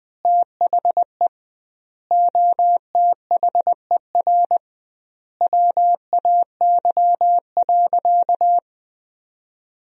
Morse code, heard as a keyed tone, about 20 wpm, 700 Hz. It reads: THE OTHER WAY.
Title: 2x Word-Spacing - Sentences from Top 100 Words - 20wpm